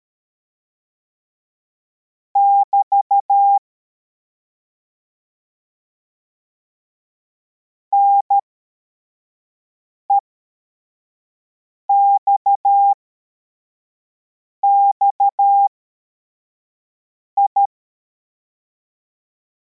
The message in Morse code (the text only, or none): <BT> NEXXI